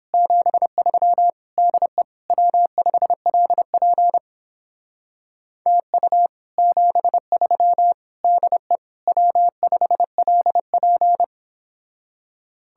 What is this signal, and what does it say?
Morse code, heard as a keyed tone, about 29 wpm, 700 Hz. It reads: 73 DE W5LP TU 73 DE W5LP